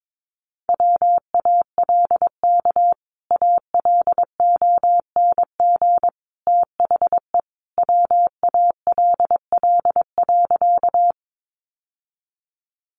Morse code, heard as a keyed tone, about 22 wpm, 700 Hz. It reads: WALK ALONG THE WALL.
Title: Sentences from Top 1100 Words 22wpm